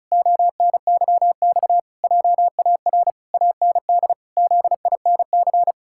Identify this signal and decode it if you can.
Morse code, heard as a keyed tone, about 36 wpm, 700 Hz. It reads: ONYX JAR AND ZINC
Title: Pangram Challenge v2 - 35wpm